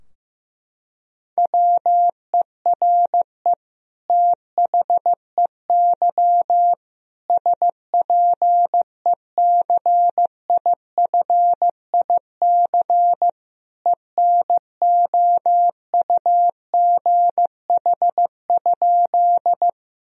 Morse code, heard as a keyed tone, about 16 wpm, 700 Hz. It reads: WERE THEY SPECIFIC ENOUGH?
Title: Sentences from Top 1000 Words - Review Part 3 - 15wpm